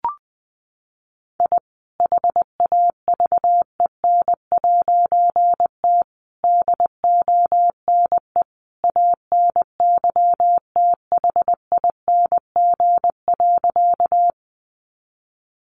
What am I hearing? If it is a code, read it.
Morse code, heard as a keyed tone, about 20 wpm, 700 Hz. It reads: I HAVEN'T DONE ANYTHING.
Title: A Sound Of Thunder 20wpm